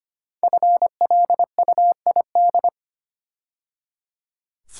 Morse code, heard as a keyed tone, about 25 wpm, 700 Hz. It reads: FLUID
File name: 5 Letter Words - First 3 Letters Given - 25wpm